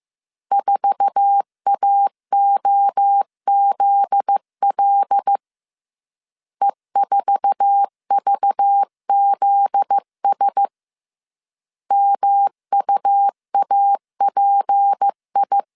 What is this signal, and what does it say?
Morse code, heard as a keyed tone, about 15 wpm, 800 Hz. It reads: 4AOZL E4VZS MUAPI
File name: CW / DET / 1 9 1500C S C 14wpm 3245